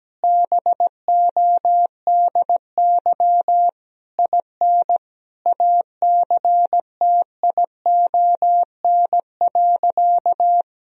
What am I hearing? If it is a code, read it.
Morse code, heard as a keyed tone, about 17 wpm, 700 Hz. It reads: BODY IN ACTION.
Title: Sentences from Top 1000 Words - Review Part 4 - 17wpm